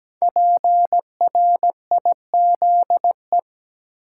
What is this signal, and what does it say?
Morse code, heard as a keyed tone, about 17 wpm, 700 Hz. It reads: PRIZE
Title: Five Letter Words - 17wpm